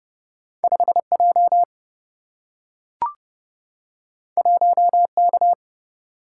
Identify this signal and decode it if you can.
Morse code, heard as a keyed tone, about 30 wpm, 700 Hz. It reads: 5J 1K
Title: Single-Letter Single-Number 30wpm